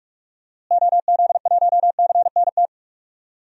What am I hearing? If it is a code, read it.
Morse code, heard as a keyed tone, about 46 wpm, 700 Hz. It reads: OZ1CNT
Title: DXCC-100 Call Signs - 6 Characters - Repeated 2x - 45wpm